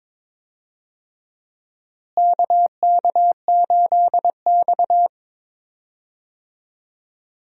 Morse code, heard as a keyed tone, about 22 wpm, 700 Hz. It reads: KK8X
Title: USA Call Signs - 4 Characters - Repeated 2x - 22wpm